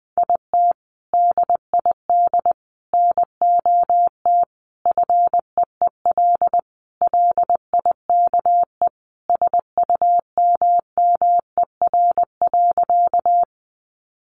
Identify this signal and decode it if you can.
Morse code, heard as a keyed tone, about 20 wpm, 700 Hz. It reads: IT DID NOT FEEL LIKE SUMMER.
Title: Sentences from Top 1100 Words 20wpm